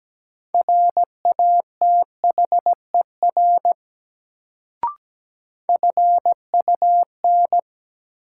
Morse code, heard as a keyed tone, about 17 wpm, 700 Hz. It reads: RATHER FUN
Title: Sentences from Top 700 Words - New Word Drill - 17wpm